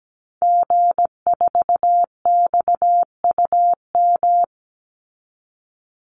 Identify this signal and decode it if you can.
Morse code, heard as a keyed tone, about 17 wpm, 700 Hz. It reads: G4XUM
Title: Call Signs - 17wpm